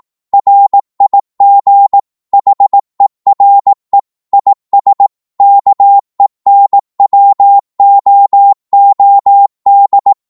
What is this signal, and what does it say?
Morse code, heard as a keyed tone, about 18 wpm, 800 Hz. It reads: RIG HERE IS KENWOOD
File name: Morse / qso-18wpm